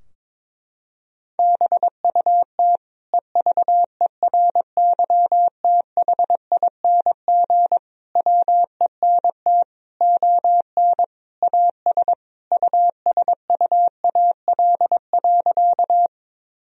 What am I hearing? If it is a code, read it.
Morse code, heard as a keyed tone, about 22 wpm, 700 Hz. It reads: BUT EVERYTHING WENT ON AS USUAL.
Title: Sentences from Top 1000 Words - Review Part 2 - 22wpm